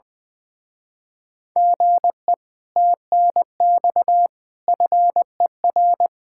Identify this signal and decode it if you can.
Morse code, heard as a keyed tone, about 19 wpm, 700 Hz. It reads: GE TNX FER
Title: Speed-Racing Common QSO Phrases - 20wpm